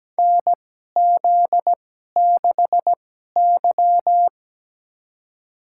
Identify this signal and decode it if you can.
Morse code, heard as a keyed tone, about 17 wpm, spaced as for 12 wpm, 700 Hz. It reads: NZ6Y
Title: USA Call Signs - 4 Characters - 2x Character Spacing - 17wpm